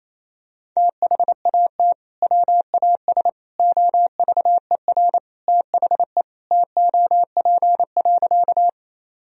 Morse code, heard as a keyed tone, about 28 wpm, 700 Hz. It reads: THAT WAS OVER THE TOP.
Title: Sentences from Top 1000 Words - Review Part 2 - 28wpm